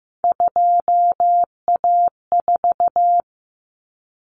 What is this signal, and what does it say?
Morse code, heard as a keyed tone, about 15 wpm, 700 Hz. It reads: 2A4
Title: Common Cut Numbers - 3-Digit 15wpm